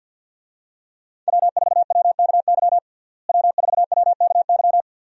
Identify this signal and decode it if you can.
Morse code, heard as a keyed tone, about 53 wpm, 700 Hz. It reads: W4WKY W4WKY
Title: USA Call Signs - 5 Characters - Repeated 2x - 50wpm